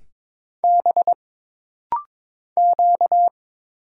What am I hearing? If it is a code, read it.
Morse code, heard as a keyed tone, about 22 wpm, 700 Hz. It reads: B Q
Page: This rapid-fire timing is geared to push your ICR (Instant Character Recognition) skills.